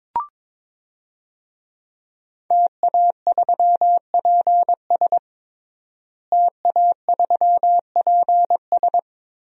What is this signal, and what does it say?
Morse code, heard as a keyed tone, about 22 wpm, 700 Hz. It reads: TA3PS TA3PS
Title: DXCC-100 Call Signs - 5 Characters - Repeated 3x - 22wpm